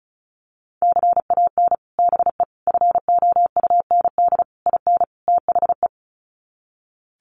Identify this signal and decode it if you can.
Morse code, heard as a keyed tone, about 34 wpm, 700 Hz. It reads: CAN BE FOUND IN THE
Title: Sets of 5 Words Encore 35wpm